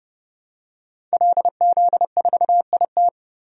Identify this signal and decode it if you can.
Morse code, heard as a keyed tone, about 30 wpm, 700 Hz. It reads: LZ4IT
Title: DXCC-100 Call Signs - 5 Characters - Repeated 2x - 30wpm